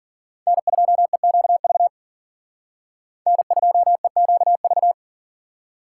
Morse code, heard as a keyed tone, about 51 wpm, 700 Hz. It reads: N2EQV N2EQV